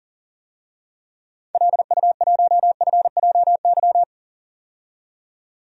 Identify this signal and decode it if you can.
Morse code, heard as a keyed tone, about 40 wpm, 700 Hz. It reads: LU1FJY
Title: DXCC-100 Call Signs - 6 Characters - 40wpm